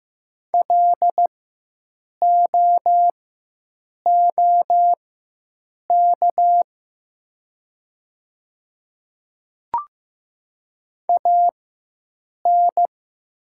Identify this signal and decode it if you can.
Morse code, heard as a keyed tone, about 15 wpm, spaced as for 7 wpm, 700 Hz. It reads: LOOK AN
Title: 4x Character Spacing - Top 100 Words - 15wpm